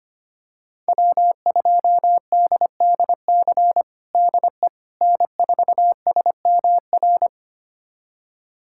Morse code, heard as a keyed tone, about 25 wpm, 700 Hz. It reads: W2DDC DE N4SMR
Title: USA Callsign DE Callsign - 5 Characters - 25wpm